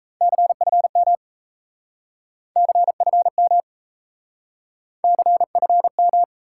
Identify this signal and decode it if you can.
Morse code, heard as a keyed tone, about 37 wpm, 700 Hz. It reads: CFM CFM CFM